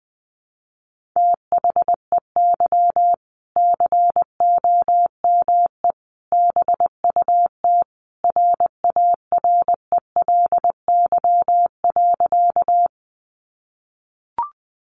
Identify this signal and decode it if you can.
Morse code, heard as a keyed tone, about 20 wpm, 700 Hz. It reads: THEY COME BUT RARELY.
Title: Sentences from Top 1400 Words 20wpm